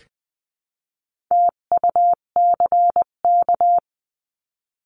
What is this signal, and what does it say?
Morse code, heard as a keyed tone, about 20 wpm, 700 Hz. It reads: TUCK
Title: 258 Lesson 42c - Words 16-20wpm